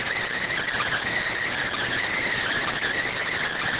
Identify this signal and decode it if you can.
Emergency Alert System (SAME) tones, heard as harsh buzzing bursts. It is heard from 0.8 to 3.4 s.